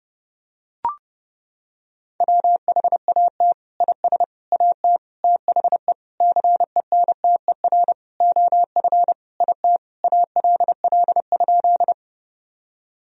Morse code, heard as a keyed tone, about 30 wpm, 700 Hz. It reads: WHAT IS AT THE CENTER OF IT ALL?
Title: Sentences from Top 1000 Words - Review Part 2 - 30wpm